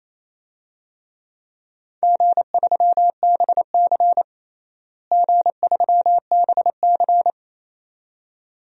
Morse code, heard as a keyed tone, about 28 wpm, 700 Hz. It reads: G3BC G3BC
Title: DXCC-100 Call Signs - 4 Characters - Repeated 2x - 28wpm